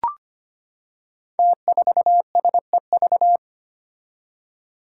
Morse code, heard as a keyed tone, about 25 wpm, 700 Hz. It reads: T4SEV
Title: Five Challenging Letter-Number Combinations 25wpm